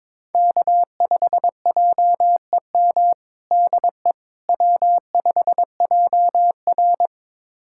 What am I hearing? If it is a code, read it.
Morse code, heard as a keyed tone, about 22 wpm, 700 Hz. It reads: K5JEM DE W5JR
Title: USA Callsign DE Callsign 22wpm